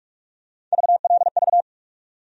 Morse code, heard as a keyed tone, about 44 wpm, 700 Hz. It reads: VLV
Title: USA Call Sign Postfixes - 3 Letters 45wpm